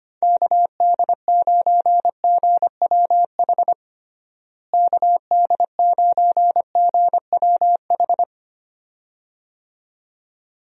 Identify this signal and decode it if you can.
Morse code, heard as a keyed tone, about 25 wpm, 700 Hz. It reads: KD9GWH KD9GWH
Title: USA Call Signs - 6 Characters - Repeated 2x - 25wpm